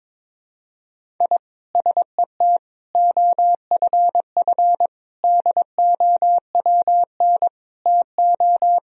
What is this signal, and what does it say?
Morse code, heard as a keyed tone, about 21 wpm, 700 Hz. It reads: I SET OFF DOWN TO
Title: Sentences from Top 1000 Words - Review Part 1 - 22wpm